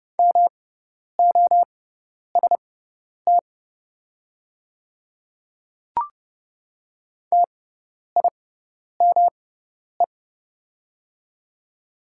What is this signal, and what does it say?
Morse code, heard as a keyed tone, about 30 wpm, spaced as for 10 wpm, 690 Hz. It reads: MOST TIME
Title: 6x Character Spacing - Top 100 Words - 30wpm